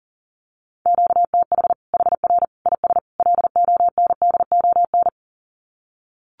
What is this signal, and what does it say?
Morse code, heard as a keyed tone, about 40 wpm, 700 Hz. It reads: QTH HR IS LONDON
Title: Common QSO Phrases 40wpm